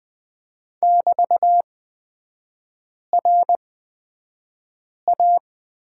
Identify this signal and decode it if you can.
Morse code, heard as a keyed tone, about 20 wpm, spaced as for 5 wpm, 700 Hz. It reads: <BT>RA
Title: 224 Lesson 36e - BT - Three Characters 20wpm